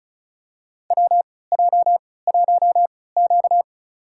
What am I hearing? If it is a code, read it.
Morse code, heard as a keyed tone, about 36 wpm, spaced as for 20 wpm, 700 Hz. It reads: WJ1Q
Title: USA Call Signs - 4 Characters - 3x Character Spacing - 35wpm